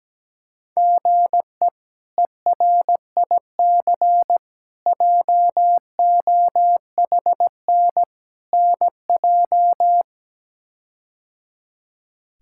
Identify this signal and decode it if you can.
Morse code, heard as a keyed tone, about 17 wpm, 700 Hz. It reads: GE ERIC JOHN NJ
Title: K1USN SST QSO Practice - 17wpm